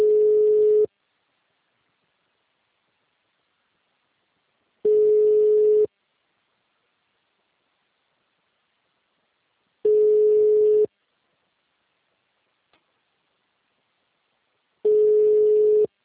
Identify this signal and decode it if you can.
Ringback tone (Europe, 425 Hz).
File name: out- -2014- - - .10058